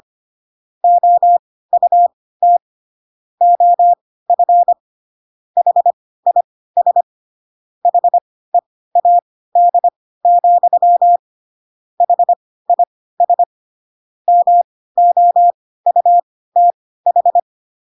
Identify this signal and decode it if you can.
Morse code, heard as a keyed tone, about 25 wpm, spaced as for 16 wpm, 700 Hz. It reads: OUT OF HIS HEAD, HIS MOUTH
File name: Pinocchio / Pinocchio ch1 10 25 ik1hge